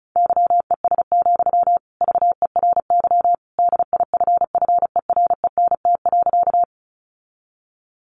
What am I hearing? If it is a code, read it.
Morse code, heard as a keyed tone, about 34 wpm, 700 Hz. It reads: YES, VERY DIFFERENT.
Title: Sentences from Top 600 Words 35wpm